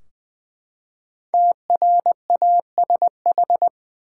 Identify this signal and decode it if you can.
Morse code, heard as a keyed tone, about 21 wpm, 700 Hz. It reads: TRASH
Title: Five Letter Words - 20wpm